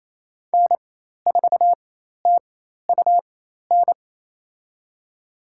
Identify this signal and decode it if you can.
Morse code, heard as a keyed tone, about 28 wpm, spaced as for 13 wpm, 690 Hz. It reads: N4TUN
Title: USA Call Signs - 5 Characters - 4x Character Spacing - 28wpm